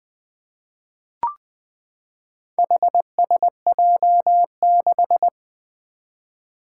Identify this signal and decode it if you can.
Morse code, heard as a keyed tone, about 20 wpm, 700 Hz. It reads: HSJ6
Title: Four Challenging Letter-Number Combinations 20wpm